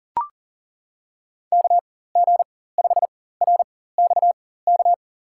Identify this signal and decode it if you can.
Morse code, heard as a keyed tone, about 37 wpm, spaced as for 18 wpm, 700 Hz. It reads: KG5RXK
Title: USA Call Signs - 6 Characters - 4x Character Spacing - 40wpm